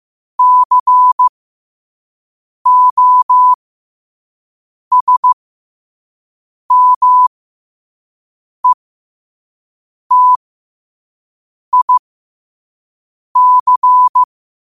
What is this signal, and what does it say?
Morse code, heard as a keyed tone, about 15 wpm, spaced as for 5 wpm, 1000 Hz. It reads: COSMETIC
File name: qotdfiles / 5wpm